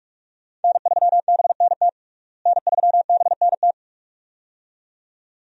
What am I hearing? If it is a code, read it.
Morse code, heard as a keyed tone, about 46 wpm, 700 Hz. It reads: N3BNT N3BNT
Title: USA Call Signs - 5 Characters - Repeated 2x - 45wpm